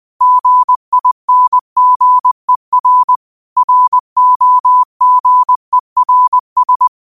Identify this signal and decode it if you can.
Morse code, heard as a keyed tone, about 20 wpm, 1000 Hz. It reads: GINGER ROGERS
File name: qotdfiles / 20wpm